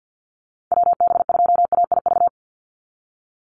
Morse code, heard as a keyed tone, about 50 wpm, 700 Hz. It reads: WB2AIV
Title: Call Signs Repeated 3x - 50wpm